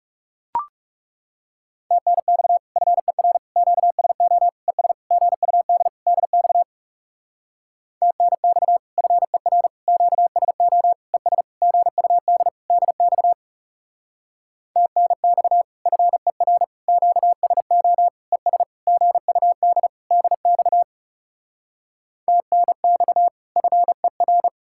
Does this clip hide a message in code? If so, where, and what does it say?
Yes: Morse code from 1.9 to 24.5 s, heard as a keyed tone, about 40 wpm, 700 Hz. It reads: TNX FER QSO ES GUD DX TNX FER QSO ES GUD DX TNX FER QSO ES GUD DX TNX FER